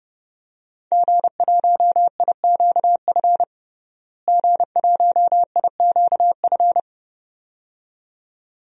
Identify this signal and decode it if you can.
Morse code, heard as a keyed tone, about 30 wpm, 700 Hz. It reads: G1IQF G1IQF
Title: DXCC-100 Call Signs - 5 Characters - Repeated 2x - 30wpm